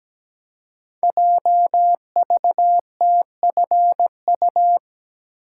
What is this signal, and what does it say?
Morse code, heard as a keyed tone, about 17 wpm, 700 Hz. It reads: JVTFU
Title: Code Groups - 5 Letter:Numbers - 17wpm